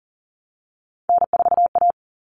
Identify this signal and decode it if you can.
Morse code, heard as a keyed tone, about 40 wpm, 700 Hz. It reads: N4A
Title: Common Cut Numbers - 3-Digit 40wpm